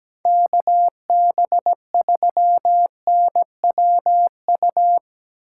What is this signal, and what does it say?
Morse code, heard as a keyed tone, about 18 wpm, 700 Hz. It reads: KB3NWU
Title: Speed Racing - USA 2x3 Call Signs 17wpm